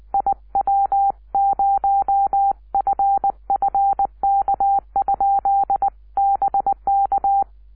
Morse code, heard as a keyed tone, about 20 wpm, 780 Hz. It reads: IW0FFK? BK
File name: iw0ffk / 2006 06 10 wq5w iw0ffk